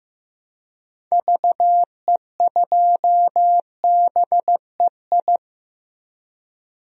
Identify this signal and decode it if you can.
Morse code, heard as a keyed tone, about 15 wpm, 700 Hz. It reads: VE2BEI
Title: DXCC-100 Call Signs - 6 Characters - Repeated 3x - 15wpm